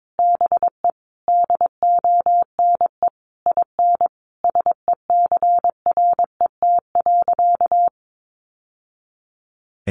Morse code, heard as a keyed tone, about 22 wpm, 700 Hz. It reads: BE DONE IN SECRET.
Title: Sentences from Top 1200 Words 22wpm